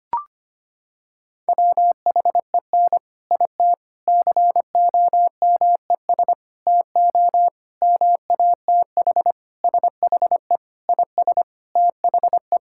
Morse code, heard as a keyed tone, about 25 wpm, 700 Hz. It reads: WHEN IT COMES TO MATH SHE IS THE
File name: practice / 25wpm / Sentences with 3-Word Phrases - 25wpm